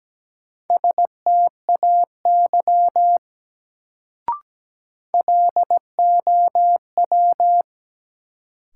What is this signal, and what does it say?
Morse code, heard as a keyed tone, about 17 wpm, 700 Hz. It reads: STAY LOW
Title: Sentences from Top 500 Words - New Word Drill - 17wpm